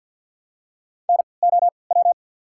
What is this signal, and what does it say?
Morse code, heard as a keyed tone, about 50 wpm, spaced as for 28 wpm, 690 Hz. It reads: NOW
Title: 3x Character Spacing - Top 200 Words - 50wpm